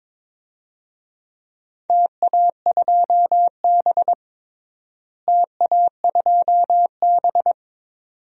Morse code, heard as a keyed tone, about 22 wpm, 700 Hz. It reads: TA2B TA2B
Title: DXCC-100 Call Signs - 4 Characters - Repeated 2x - 22wpm